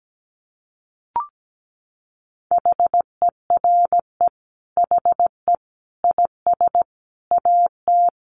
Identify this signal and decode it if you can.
Morse code, heard as a keyed tone, about 17 wpm, 700 Hz. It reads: HERE HE IS AT
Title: Sentences from Top 300 Words 17wpm